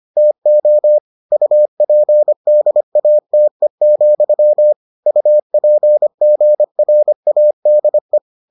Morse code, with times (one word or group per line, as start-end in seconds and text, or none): 0.2-1.0 TO
1.3-4.7 UPDATE,
5.1-8.2 UPGRADE